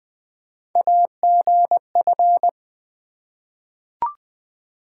Morse code, heard as a keyed tone, about 20 wpm, 700 Hz. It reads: AGF
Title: Three Letter-Number Combinations - 20wpm